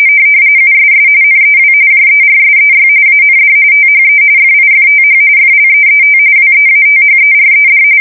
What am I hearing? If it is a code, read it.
RTTY, about 45.45 baud. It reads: USS NEW JERSEY AND THE FIFTH FLEET WEATHERED
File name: fsk170 / battleshipnj